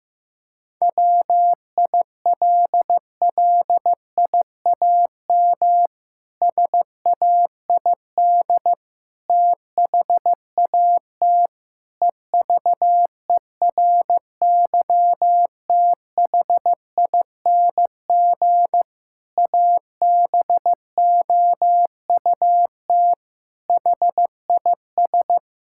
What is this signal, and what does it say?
Morse code, heard as a keyed tone, about 15 wpm, 700 Hz. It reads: WILLIAM SAID THAT EVERYTHING ABOUT HIS
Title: Pangram Challenge v2 - 15wpm